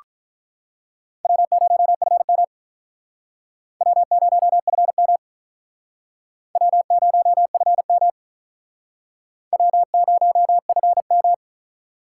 Morse code, heard as a keyed tone, about 45 wpm, 700 Hz. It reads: W0FM W0FM W0FM W0FM